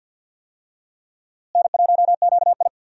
Morse code, heard as a keyed tone, about 48 wpm, 700 Hz. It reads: N1QI
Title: USA Call Signs - 4 Characters - Repeated 3x - 50wpm